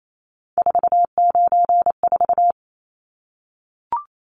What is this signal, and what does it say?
Morse code, heard as a keyed tone, about 28 wpm, 700 Hz. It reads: 494
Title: Numbers - 3-Digit 28wpm